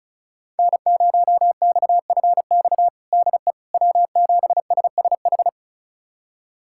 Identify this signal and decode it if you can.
Morse code, heard as a keyed tone, about 36 wpm, 700 Hz. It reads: N0XFX DE W7SSH
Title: USA Callsign DE Callsign - 5 Characters - 35wpm